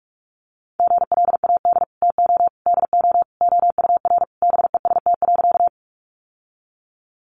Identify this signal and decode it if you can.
Morse code, heard as a keyed tone, about 44 wpm, 700 Hz. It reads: GLAD TO DO OUR BEST.
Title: Sentences from Top 800 Words 45wpm